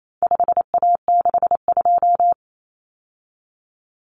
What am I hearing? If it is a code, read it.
Morse code, heard as a keyed tone, about 28 wpm, 700 Hz. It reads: 5A62
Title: Common Cut Numbers - 4-Digit 28wpm